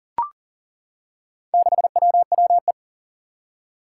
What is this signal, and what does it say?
Morse code, heard as a keyed tone, about 40 wpm, 700 Hz. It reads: 6WWE
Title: Four Challenging Letter-Number Combinations 40wpm